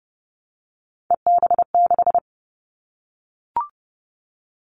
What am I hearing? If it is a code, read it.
Morse code, heard as a keyed tone, about 30 wpm, 700 Hz. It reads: EB6
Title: Callsign Prefixes for Top 100 DXCC - 30wpm